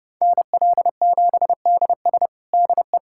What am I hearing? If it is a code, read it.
Morse code, heard as a keyed tone, about 30 wpm, 700 Hz. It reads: NL7DS DE
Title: USA Callsign DE Callsign - 5 Characters - 30wpm